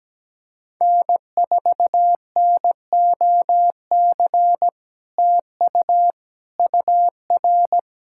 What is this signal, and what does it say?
Morse code, heard as a keyed tone, about 17 wpm, 700 Hz. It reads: N4NOC TU UR
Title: SOTA QSO Practice 17wpm